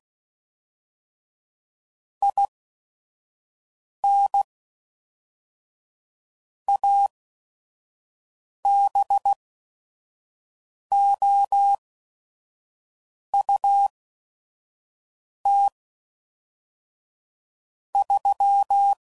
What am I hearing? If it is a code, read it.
Morse code, heard as a keyed tone, about 16 wpm, spaced as for 5 wpm, 800 Hz. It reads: IN ABOUT 3
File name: CW Code Groups @5WPM / QSOs / WD0FBT 1